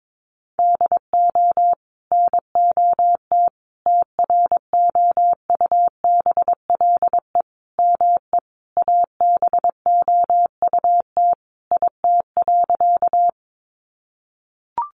Morse code, heard as a keyed tone, about 22 wpm, 700 Hz. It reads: DO NOT TROUBLE ME ABOUT IT.
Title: Sentences from Top 1200 Words 22wpm